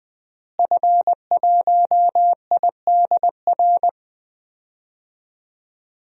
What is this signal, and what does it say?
Morse code, heard as a keyed tone, about 20 wpm, 700 Hz. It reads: F1IDR
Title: DXCC-100 Call Signs - 5 Characters - Repeated 3x - 20wpm